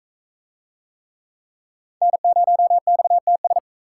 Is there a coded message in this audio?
Yes: Morse code from 2.0 to 3.6 s, heard as a keyed tone, about 39 wpm, 700 Hz. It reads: N0XTS